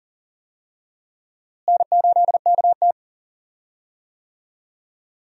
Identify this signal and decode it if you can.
Morse code, heard as a keyed tone, about 40 wpm, 700 Hz. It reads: N8KT
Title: USA Call Signs - 4 Characters - Repeated 2x - 40wpm